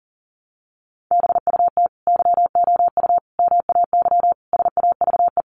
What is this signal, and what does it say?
Morse code, heard as a keyed tone, about 40 wpm, 700 Hz. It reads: BUT YOU MAY SAVE
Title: Sentences from Top 1100 Words 40wpm